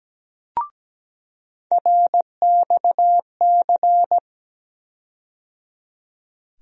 Morse code, heard as a keyed tone, about 17 wpm, 700 Hz. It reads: RXC
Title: USA Call Sign Postfixes - 3 Letters 17wpm